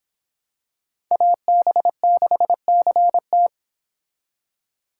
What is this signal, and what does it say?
Morse code, heard as a keyed tone, about 25 wpm, 700 Hz. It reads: AB6CT